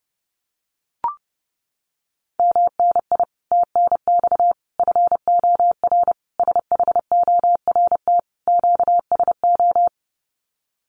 Morse code, heard as a keyed tone, about 30 wpm, 700 Hz. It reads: MNI TNX FOR SHORT QSO
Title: Common QSO Phrases 30wpm